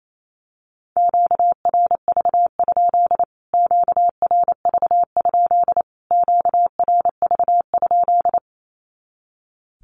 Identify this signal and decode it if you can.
Morse code, heard as a keyed tone, about 28 wpm, 700 Hz. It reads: QRV? QRV? QRV?
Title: Q-Codes Repeated 3x 28wpm